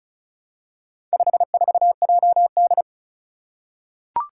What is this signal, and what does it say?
Morse code, heard as a keyed tone, about 36 wpm, 700 Hz. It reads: 54JD
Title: Four Challenging Letter-Number Combinations 35wpm